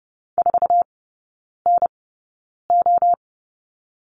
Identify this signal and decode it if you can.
Morse code, heard as a keyed tone, about 30 wpm, spaced as for 9 wpm, 700 Hz. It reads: 4NO
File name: Lesson 18e - Letter m - Three Characters 30wpm